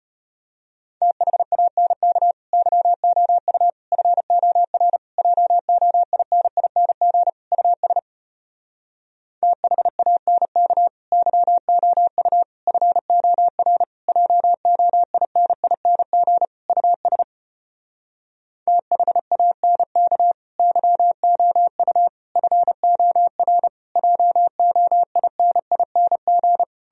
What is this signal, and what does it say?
Morse code, heard as a keyed tone, about 33 wpm, 700 Hz. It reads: THANK YOU FOR JOINING US THANK YOU FOR JOINING US THANK YOU FOR JOINING